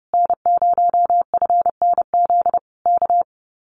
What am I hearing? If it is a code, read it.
Morse code, heard as a keyed tone, about 30 wpm, 700 Hz. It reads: N0FNZ K
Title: Common QSO Phrases Encore 30wpm